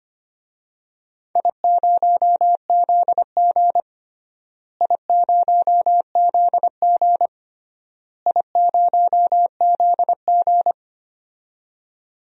Morse code, heard as a keyed tone, about 25 wpm, 700 Hz. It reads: I0ZG I0ZG I0ZG
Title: DXCC-100 Call Signs - 4 Characters - Repeated 3x - 25wpm